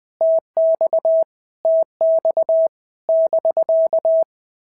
Morse code, heard as a keyed tone, about 20 wpm, 650 Hz. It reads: TX TX <BK>
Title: POTA QSO Practice - 20wpm